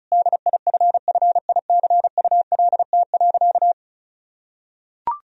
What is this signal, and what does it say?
Morse code, heard as a keyed tone, about 36 wpm, 700 Hz. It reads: DIFFICULT.
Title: Sentences from Top 1000 Words - Review Part 2 - 35wpm